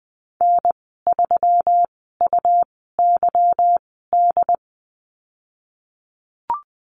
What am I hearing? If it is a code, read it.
Morse code, heard as a keyed tone, about 20 wpm, spaced as for 15 wpm, 700 Hz. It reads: N3UYD
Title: 193 Lesson 31g - Letter v - Call Signs 20wpm